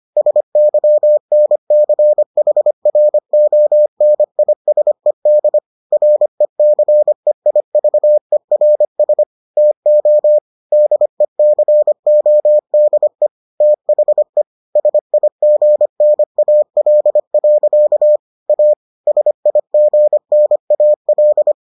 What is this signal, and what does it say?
Morse code, heard as a keyed tone, about 25 wpm, 600 Hz. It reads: SYNCHRONISED RECEIVERS TO DECODE THE SIGNAL. A SIGNAL